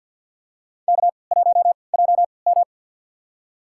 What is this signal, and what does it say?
Morse code, heard as a keyed tone, about 53 wpm, spaced as for 29 wpm, 700 Hz. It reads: K1JM
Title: USA Call Signs - 4 Characters - 3x Character Spacing - 50wpm